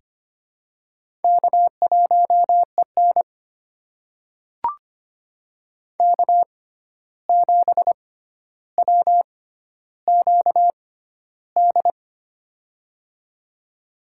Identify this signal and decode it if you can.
Morse code, heard as a keyed tone, about 25 wpm, spaced as for 12 wpm, 700 Hz. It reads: K1EN K7WQD
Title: USA Call Signs - 6x Character Spacing - 25wpm